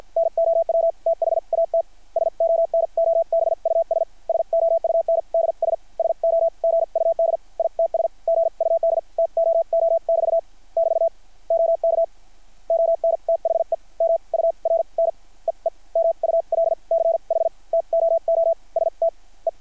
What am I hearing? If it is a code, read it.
Morse code, heard as a keyed tone, about 53 wpm, spaced as for 39 wpm, 660 Hz. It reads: NOW THATS ONOBUS SOUNDS SO GUD ITS GUD TOO <BT> <BT> OK ON THE MURNEE MULCH TOOSTE
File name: archived / CRSnet